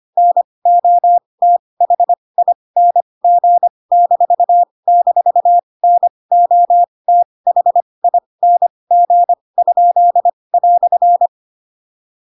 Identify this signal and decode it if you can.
Morse code, heard as a keyed tone, about 25 wpm, spaced as for 20 wpm, 700 Hz. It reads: NOTHING--NOTHING?"
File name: Pinocchio / Pinocchio ch7 15 25 ik1hge